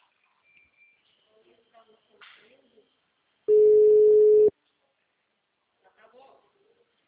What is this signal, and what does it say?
Ringback tone (Europe, 425 Hz), with speech.
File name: out- -2001- - - .64924